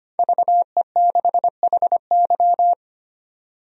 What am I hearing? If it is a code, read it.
Morse code, heard as a keyed tone, about 25 wpm, 700 Hz. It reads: VE6HY
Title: DXCC-100 Call Signs - 5 Characters - 25wpm